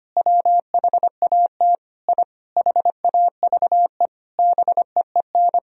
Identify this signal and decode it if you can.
Morse code, heard as a keyed tone, about 25 wpm, 700 Hz. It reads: WHAT I HAVE BEEN
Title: Sentences from Top 1000 Words - Review Part 1 - 25wpm